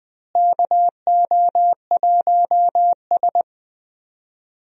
Morse code, heard as a keyed tone, about 21 wpm, 700 Hz. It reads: KO1S
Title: USA Call Signs - 4 Characters - Repeated 3x - 20wpm